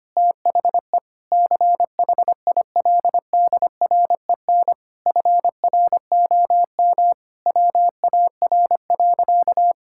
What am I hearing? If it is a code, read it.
Morse code, heard as a keyed tone, about 25 wpm, 700 Hz. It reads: THE CHILDREN FROM WAR.
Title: Sentences from Top 1000 Words - Review Part 6 - 25wpm